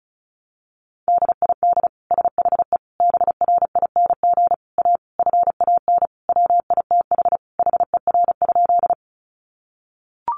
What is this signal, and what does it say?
Morse code, heard as a keyed tone, about 34 wpm, 700 Hz. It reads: DID SHE BRING A FAN WITH HER?
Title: Sentences from Top 1300 Words 35wpm